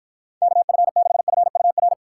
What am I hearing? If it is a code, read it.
Morse code, heard as a keyed tone, about 53 wpm, 700 Hz. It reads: KF6FUR